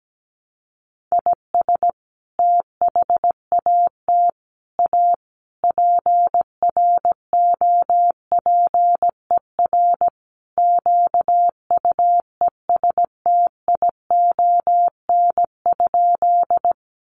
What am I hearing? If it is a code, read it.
Morse code, heard as a keyed tone, about 17 wpm, 700 Hz. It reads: IS THAT A PROPER QUESTION?
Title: Sentences from Top 1100 Words 17wpm